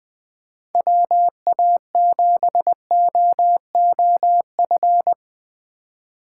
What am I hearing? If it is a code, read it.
Morse code, heard as a keyed tone, about 21 wpm, 700 Hz. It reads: WA7OOF
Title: Speed Racing - USA Call Signs 17wpm